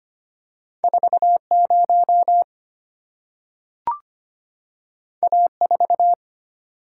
Morse code, heard as a keyed tone, about 25 wpm, 700 Hz. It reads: 40 A4
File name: Two Letter-Number Combinations - 25wpm